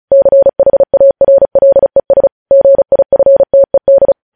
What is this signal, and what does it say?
Morse code, heard as a keyed tone, about 36 wpm, 550 Hz. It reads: CHARLES GIFTED
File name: podcast / News headlines-1x-35x35